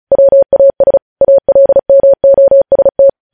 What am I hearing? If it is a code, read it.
Morse code, heard as a keyed tone, about 36 wpm, 550 Hz. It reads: WAS ALMOST
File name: podcast / News headlines-1x-35x35